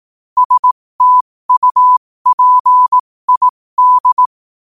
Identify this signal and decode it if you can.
Morse code, heard as a keyed tone, about 18 wpm, spaced as for 15 wpm, 1000 Hz. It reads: STUPID
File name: qotdfiles / 15wpm